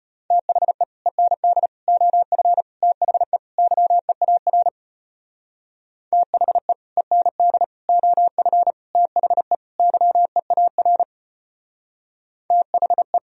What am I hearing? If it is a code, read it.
Morse code, heard as a keyed tone, about 32 wpm, 700 Hz. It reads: THE END OF THE YEAR THE END OF THE YEAR THE